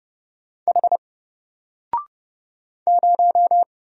Morse code, heard as a keyed tone, about 29 wpm, 700 Hz. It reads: H 0
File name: Single-Letter-Number-30wpm